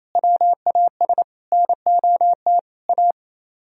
Morse code, heard as a keyed tone, about 34 wpm, 700 Hz. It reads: WAS NOT A
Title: Speed-Racing - Sets of 4 Words 28wpm